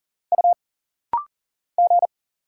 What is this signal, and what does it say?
Morse code, heard as a keyed tone, about 40 wpm, 700 Hz. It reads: U G
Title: Single Letters - Rapid-Fire 40wpm